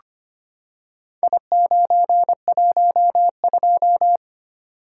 Morse code, heard as a keyed tone, about 25 wpm, 700 Hz. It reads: I912
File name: practice / 25wpm / Four Letter-Number Combinations - 25wpm